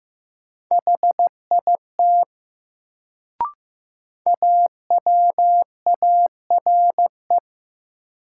Morse code, heard as a keyed tone, about 15 wpm, 700 Hz. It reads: HIT AWARE
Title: Sentences from Top 1200 Words - New Word Drill - 15wpm